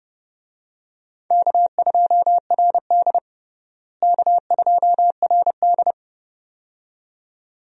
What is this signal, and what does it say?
Morse code, heard as a keyed tone, about 30 wpm, 700 Hz. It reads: K2RD K2RD
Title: USA Call Signs - 4 Characters - Repeated 2x - 30wpm